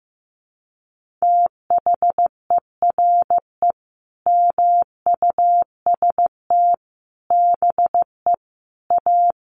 Morse code, heard as a keyed tone, about 15 wpm, 700 Hz. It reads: THERE MUST BE A
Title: Sentences from Top 1400 Words 15wpm